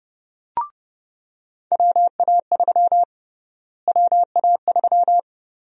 Morse code, heard as a keyed tone, about 30 wpm, 700 Hz. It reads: WA3 WA3
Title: USA Call Sign Prefixes - Repeated 2x - 30wpm